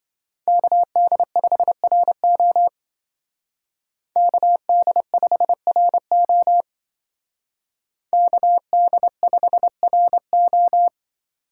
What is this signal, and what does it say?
Morse code, heard as a keyed tone, about 26 wpm, 700 Hz. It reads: KD5RO KD5RO KD5RO